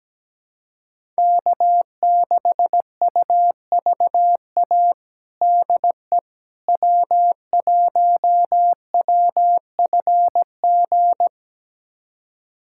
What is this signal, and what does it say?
Morse code, heard as a keyed tone, about 17 wpm, 700 Hz. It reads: K6UVA DE W1WFG
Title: USA Callsign DE Callsign - 5 Characters - 17wpm